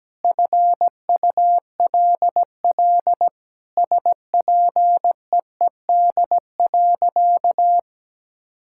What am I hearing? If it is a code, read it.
Morse code, heard as a keyed tone, about 17 wpm, 700 Hz. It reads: FULL SPEED.
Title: Sentences from Top 1000 Words - Review Part 5 - 17wpm